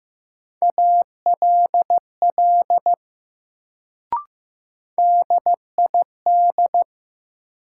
Morse code, heard as a keyed tone, about 15 wpm, 700 Hz. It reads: ALL DID
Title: Sentences from Top 100 Words - New Word Drill - 15wpm